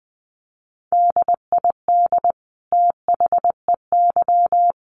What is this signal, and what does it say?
Morse code, heard as a keyed tone, about 20 wpm, 700 Hz. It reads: DID THEY
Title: Sentences from Top 1400 Words 20wpm